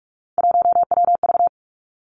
Morse code, heard as a keyed tone, about 46 wpm, 700 Hz. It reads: 1WV
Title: Three Challenging Letter-Number Combinations 45wpm